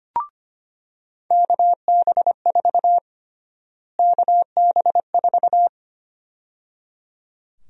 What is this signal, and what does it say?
Morse code, heard as a keyed tone, about 25 wpm, 700 Hz. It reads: KB4 KB4
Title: USA Call Sign Prefixes - Repeated 2x - 25wpm